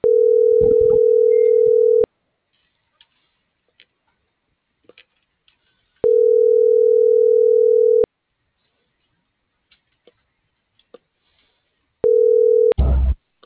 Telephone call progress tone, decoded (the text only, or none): ringback tone